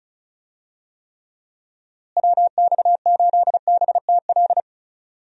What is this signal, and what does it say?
Morse code, heard as a keyed tone, about 36 wpm, 700 Hz. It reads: WX8BTL
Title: USA Call Signs - 6 Characters - Repeated 3x - 35wpm